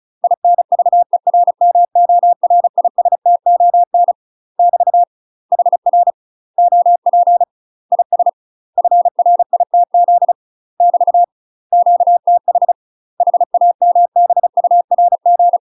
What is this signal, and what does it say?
Morse code, heard as a keyed tone, about 37 wpm, 700 Hz. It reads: INVERMORISTON <BT> HR OP IS FRITZ <BT> QTH HAMBURG
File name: UK MASTER QSOs TRAINING / 35WPM / UK MASTER QSOs 0003